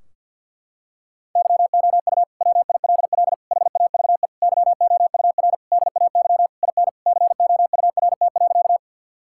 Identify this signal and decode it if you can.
Morse code, heard as a keyed tone, about 52 wpm, 700 Hz. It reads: YOU WILL HAVE YOUR DAY IN COURT.
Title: Sentences from Top 1600 Words 50wpm